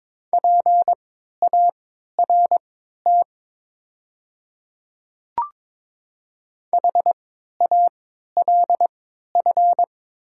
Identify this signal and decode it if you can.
Morse code, heard as a keyed tone, about 23 wpm, spaced as for 13 wpm, 700 Hz. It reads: PART HALF
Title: 3x Character Spacing - Top 500 Words - 22wpm